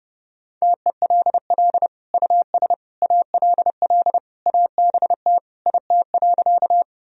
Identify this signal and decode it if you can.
Morse code, heard as a keyed tone, about 30 wpm, 700 Hz. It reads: TELL US ALL ABT IT.
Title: Sentences from Top 200 Words and Abbreviations 30wpm